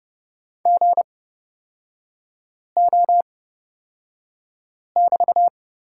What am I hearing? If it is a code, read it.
Morse code, heard as a keyed tone, about 30 wpm, spaced as for 5 wpm, 700 Hz. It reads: GO<BT>
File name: Lesson 37e - Letter x - Three Characters 30wpm